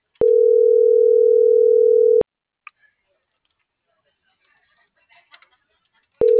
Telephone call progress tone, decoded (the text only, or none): ringback tone